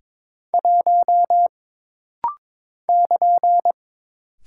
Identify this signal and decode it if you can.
Morse code, heard as a keyed tone, about 22 wpm, 700 Hz. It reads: 1 <KN>
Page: This rapid-fire timing is geared to push your ICR (Instant Character Recognition) skills.